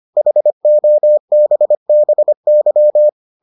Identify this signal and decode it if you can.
Morse code, heard as a keyed tone, about 25 wpm, 600 Hz. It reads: HOBBY